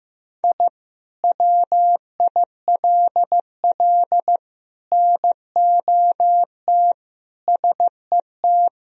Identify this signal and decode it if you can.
Morse code, heard as a keyed tone, about 15 wpm, 700 Hz. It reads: I WILL NOT SET
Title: Sentences from Top 1000 Words - Review Part 1 - 15wpm